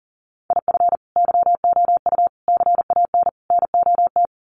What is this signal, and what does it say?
Morse code, heard as a keyed tone, about 40 wpm, 700 Hz. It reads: IF YOU CAN NOT
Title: Sets of 4 Words 40wpm